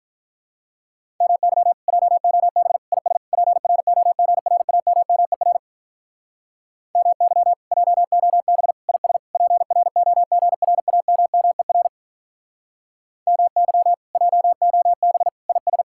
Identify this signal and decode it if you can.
Morse code, heard as a keyed tone, about 46 wpm, 700 Hz. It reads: MY JOB IS PROGRAMMER MY JOB IS PROGRAMMER MY JOB IS